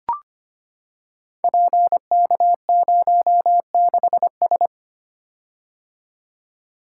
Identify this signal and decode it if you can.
Morse code, heard as a keyed tone, about 25 wpm, 700 Hz. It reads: PK06S
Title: Code Groups - 5 Letter:Numbers - 25wpm